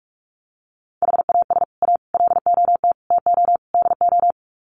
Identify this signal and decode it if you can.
Morse code, heard as a keyed tone, about 44 wpm, 700 Hz. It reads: HAS A LOT TO DO
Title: Sets of 5 Words Encore2 45wpm